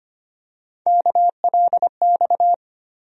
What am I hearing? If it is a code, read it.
Morse code, heard as a keyed tone, about 25 wpm, 700 Hz. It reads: KLX
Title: USA Call Sign Postfixes - 3 Letters 25wpm